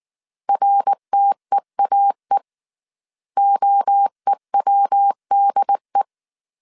Morse code, heard as a keyed tone, about 19 wpm, 800 Hz. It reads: RTEAE OEWDE